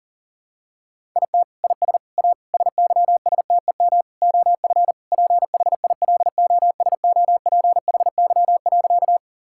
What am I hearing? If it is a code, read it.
Morse code, heard as a keyed tone, about 37 wpm, 690 Hz. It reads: IT IS A SYSTEM OF PHILOSOPHY.
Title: Sentences from Top 1000 Words - Review Part 6 - 40wpm